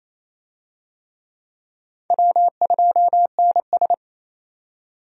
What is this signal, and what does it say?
Morse code, heard as a keyed tone, about 28 wpm, 700 Hz. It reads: W2NS
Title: USA Call Signs - 4 Characters - 28wpm